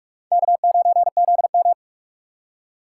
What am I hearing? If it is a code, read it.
Morse code, heard as a keyed tone, about 46 wpm, 700 Hz. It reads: K9ZM
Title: USA Call Signs - 4 Characters - 6x Character Spacing - 45wpm